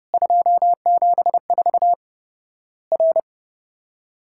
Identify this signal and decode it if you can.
Morse code, heard as a keyed tone, about 31 wpm, 700 Hz. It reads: 274 R
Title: Worked All Europe DX Contest - QTC Exchange - 30wpm